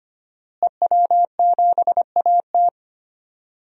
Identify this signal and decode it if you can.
Morse code, heard as a keyed tone, about 25 wpm, 700 Hz. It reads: EW7AT
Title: DXCC-100 Call Signs - 5 Characters - Repeated 3x - 25wpm